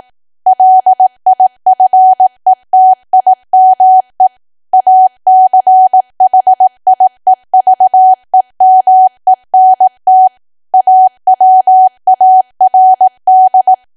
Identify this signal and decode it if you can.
Morse code, heard as a keyed tone, about 18 wpm, 750 Hz. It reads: LIFETIME ACHIEVEMENT AWARD
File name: Archive / 18 WPM / 18WPM